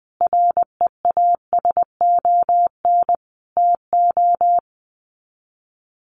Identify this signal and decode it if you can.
Morse code, heard as a keyed tone, about 20 wpm, 700 Hz. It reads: REASON TO
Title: Sets of 5 Words Encore 20wpm